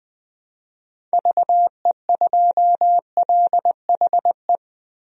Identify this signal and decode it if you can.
Morse code, heard as a keyed tone, about 21 wpm, 700 Hz. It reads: VE2LHE
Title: DXCC-100 Call Signs - 6 Characters - 20wpm